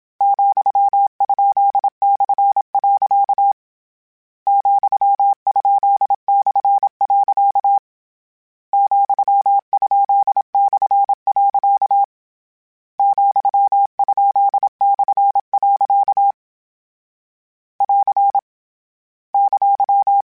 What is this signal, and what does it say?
Morse code, heard as a keyed tone, about 27 wpm, 800 Hz. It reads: ,?/. ,?/. ,?/. ,?/. <AR> !
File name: Misc / Misc-mp3s / Punct @25